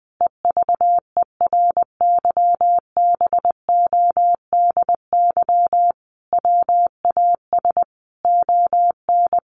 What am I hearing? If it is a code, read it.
Morse code, heard as a keyed tone, about 20 wpm, 700 Hz. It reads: EVERYBODY WAS ON
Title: Sentences from Top 1400 Words 20wpm